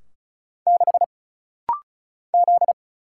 Morse code, heard as a keyed tone, about 35 wpm, 700 Hz. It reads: 6 Z
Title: Single Letter-Number Alt - Rapid-Fire 35wpm